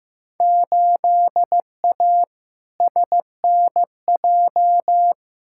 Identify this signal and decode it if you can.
Morse code, heard as a keyed tone, about 16 wpm, 700 Hz. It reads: 8A SNJ
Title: Field Day Exchanges Encore - 15wpm